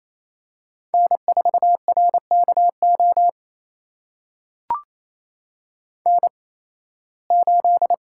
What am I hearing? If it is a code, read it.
Morse code, heard as a keyed tone, about 28 wpm, spaced as for 18 wpm, 700 Hz. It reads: N4RKO N8
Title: USA Call Signs - 5 Characters - 8x Character Spacing - 28wpm